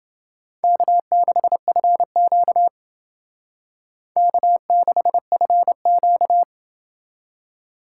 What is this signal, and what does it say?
Morse code, heard as a keyed tone, about 29 wpm, 700 Hz. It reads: K6FQ K6FQ